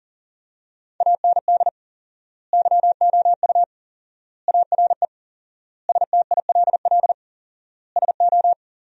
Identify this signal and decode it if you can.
Morse code, heard as a keyed tone, about 40 wpm, 700 Hz. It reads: AND YOU ARE STILL SO
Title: 4x Word-Spacing - Sentences from Top 400 Words - 40wpm